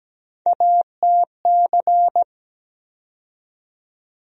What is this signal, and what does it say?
Morse code, heard as a keyed tone, about 17 wpm, 700 Hz. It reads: ATC
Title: USA Call Sign Postfixes - 3 Letters 17wpm